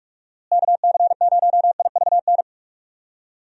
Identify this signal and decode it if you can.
Morse code, heard as a keyed tone, about 43 wpm, 700 Hz. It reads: KC0IVN
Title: USA Call Signs - 6 Characters - Repeated 2x - 45wpm